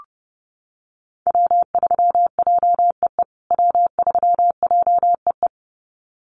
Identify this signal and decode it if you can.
Morse code, heard as a keyed tone, about 28 wpm, 700 Hz. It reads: W3JEE W3JEE
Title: USA Call Signs Repeated 2x 30wpm